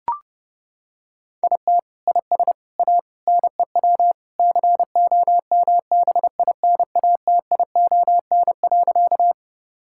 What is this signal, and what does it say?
Morse code, heard as a keyed tone, about 32 wpm, 700 Hz. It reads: IT IS A NEW COMBINATION.
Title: Sentences from Top 1000 Words - Review Part 6 - 30wpm